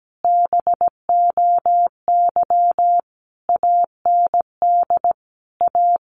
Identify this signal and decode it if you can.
Morse code, heard as a keyed tone, about 17 wpm, 700 Hz. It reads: BOY AND A
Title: Sentences from Top 1200 Words 17wpm